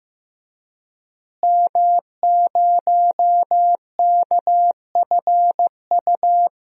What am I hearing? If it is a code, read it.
Morse code, heard as a keyed tone, about 15 wpm, 700 Hz. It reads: M0KFU
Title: DXCC-100 Call Signs - 5 Characters - Repeated 2x - 15wpm